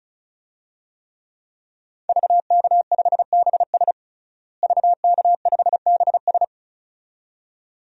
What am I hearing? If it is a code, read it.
Morse code, heard as a keyed tone, about 36 wpm, 700 Hz. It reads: VK5BS VK5BS
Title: DXCC-100 Call Signs - 5 Characters - Repeated 2x - 35wpm